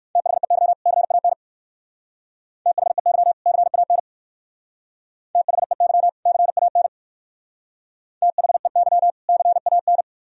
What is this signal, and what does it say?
Morse code, heard as a keyed tone, about 58 wpm, 700 Hz. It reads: THEY CAN THEY CAN THEY CAN THEY CAN